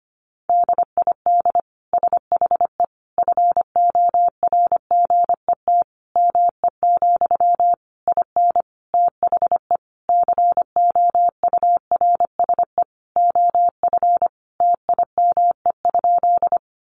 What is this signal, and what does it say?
Morse code, heard as a keyed tone, about 25 wpm, 700 Hz. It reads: DID SHE FORGET ME, IN THE COURSE OF TIME?
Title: Sentences from Top 1200 Words 25wpm